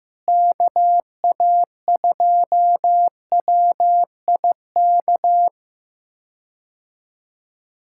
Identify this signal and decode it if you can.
Morse code, heard as a keyed tone, about 15 wpm, 700 Hz. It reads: KA2WIK
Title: K1USN SST QSO Practice - 15wpm